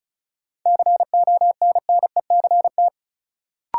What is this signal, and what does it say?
Morse code, heard as a keyed tone, about 36 wpm, 700 Hz. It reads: CONNECT
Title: Sentences from Top 1500 Words - New Word Drill Alt - 35wpm